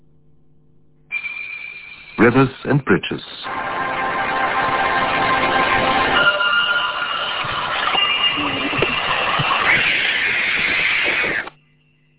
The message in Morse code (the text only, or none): none